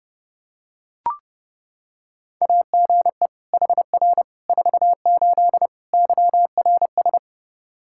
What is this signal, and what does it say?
Morse code, heard as a keyed tone, about 30 wpm, 700 Hz. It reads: AGE HR 48 YRS
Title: Practice QSO - Exchange 3 - More Info - 30wpm